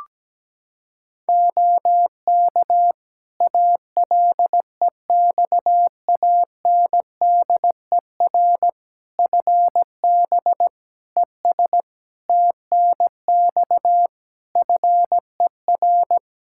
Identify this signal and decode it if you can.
Morse code, heard as a keyed tone, about 17 wpm, 700 Hz. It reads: OK ALEXANDER FB ES TNX FER
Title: K1USN SST QSO Practice 17wpm